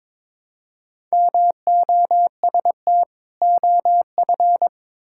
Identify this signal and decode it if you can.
Morse code, heard as a keyed tone, about 22 wpm, 700 Hz. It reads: MOST OF
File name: practice / 22wpm / Sentences with 4-Word Phrases - 22wpm